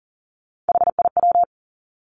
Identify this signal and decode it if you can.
Morse code, heard as a keyed tone, about 39 wpm, 700 Hz. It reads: HIW
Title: Three Challenging Letter-Number Combinations 40wpm